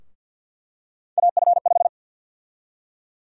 Morse code, heard as a keyed tone, about 48 wpm, 700 Hz. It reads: AF5
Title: USA Call Sign Prefixes - 50wpm